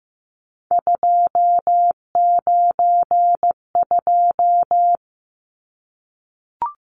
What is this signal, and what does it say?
Morse code, heard as a keyed tone, about 15 wpm, 700 Hz. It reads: 292
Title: Numbers - 3-Digit 15wpm